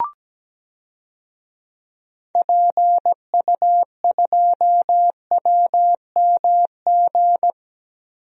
Morse code, heard as a keyed tone, about 17 wpm, 700 Hz. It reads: PU2WMG
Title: DXCC-100 Call Signs - 6 Characters - Repeated 2x - 17wpm